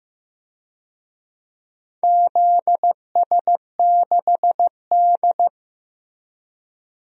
Morse code, heard as a keyed tone, about 15 wpm, 700 Hz. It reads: ZS6D
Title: DXCC-100 Call Signs - 4 Characters - 15wpm